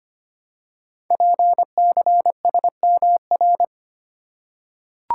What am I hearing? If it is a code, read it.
Morse code, heard as a keyed tone, about 25 wpm, 700 Hz. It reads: PCSMR
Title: Code Groups - 5 Letter:Numbers - 25wpm